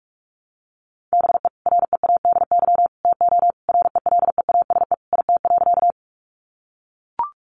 Morse code, heard as a keyed tone, about 44 wpm, 700 Hz. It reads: BE READY TO RELEASE IT.
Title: Sentences from Top 1100 Words 45wpm